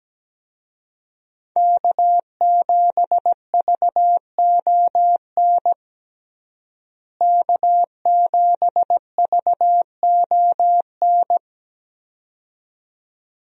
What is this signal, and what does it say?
Morse code, heard as a keyed tone, about 17 wpm, 700 Hz. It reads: K7VON K7VON
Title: USA Call Signs - 5 Characters - Repeated 2x - 17wpm